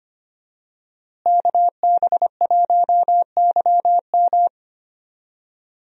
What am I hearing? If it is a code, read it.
Morse code, heard as a keyed tone, about 25 wpm, 700 Hz. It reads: KB1YM
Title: USA Call Signs - 5 Characters - Repeated 2x - 25wpm